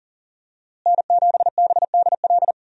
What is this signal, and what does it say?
Morse code, heard as a keyed tone, about 40 wpm, 700 Hz. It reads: N7BDL
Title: USA Call Signs - 5 Characters - Repeated 3x - 40wpm